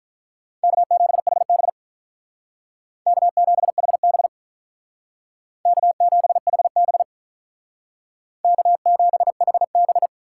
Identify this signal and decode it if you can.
Morse code, heard as a keyed tone, about 47 wpm, 700 Hz. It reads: K7HB K7HB K7HB K7HB